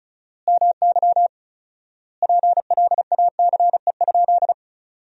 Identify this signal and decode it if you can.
Morse code, heard as a keyed tone, about 35 wpm, 700 Hz. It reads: MY PLACE?
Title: 4x Word-Spacing - Sentences from Top 200 Words - 35wpm